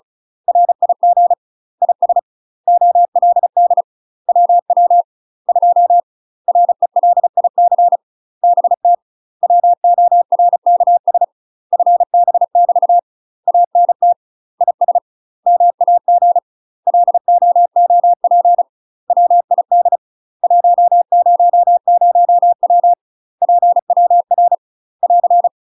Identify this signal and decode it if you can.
Morse code, heard as a keyed tone, about 36 wpm, 700 Hz. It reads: RIG IS OLD WW 2 RELIC BT WORKS FB <BT> ANT IS MAG LOOP WID 100W PWR <AR>
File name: UK MASTER QSOs TRAINING/35WPM